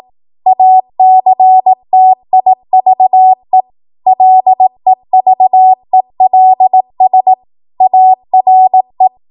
Morse code, heard as a keyed tone, about 18 wpm, 750 Hz. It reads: ACTIVE LEVELS ARE
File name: Morse / 18WPM